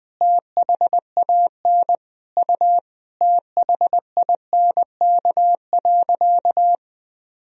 Morse code, heard as a keyed tone, about 20 wpm, 700 Hz. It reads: THAN U THINK.
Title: Sentences from Top 400 Words and Abbreviations 20wpm